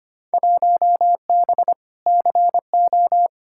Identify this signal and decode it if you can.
Morse code, heard as a keyed tone, about 25 wpm, 700 Hz. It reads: 1B CO
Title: Field Day Exchanges - 25wpm